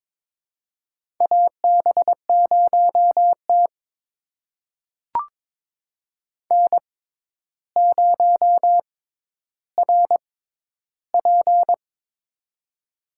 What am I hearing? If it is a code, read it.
Morse code, heard as a keyed tone, about 22 wpm, spaced as for 11 wpm, 700 Hz. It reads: AB0T N0RP
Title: USA Call Signs - 4 Characters - 6x Character Spacing - 22wpm